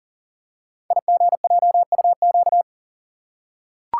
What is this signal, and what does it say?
Morse code, heard as a keyed tone, about 39 wpm, 700 Hz. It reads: IGJUQ
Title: Code Groups - 5 Letters - 40wpm